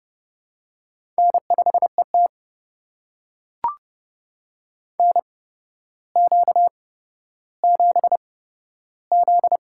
Morse code, heard as a keyed tone, about 28 wpm, spaced as for 13 wpm, 700 Hz. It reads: N5ET NQ7Z
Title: USA Call Signs - 4 Characters - 8x Character Spacing - 30wpm